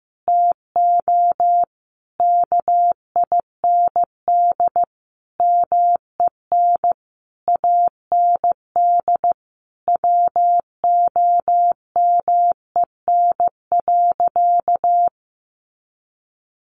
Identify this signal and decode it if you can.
Morse code, heard as a keyed tone, about 15 wpm, 700 Hz. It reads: TO KIND MEN AND WOMEN.
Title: Sentences from Top 1300 Words 15wpm